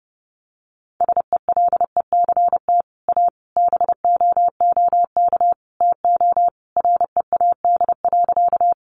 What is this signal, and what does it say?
Morse code, heard as a keyed tone, about 30 wpm, 700 Hz. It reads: SELECT A BOOK TO READ.
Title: Sentences from Top 1300 Words 30wpm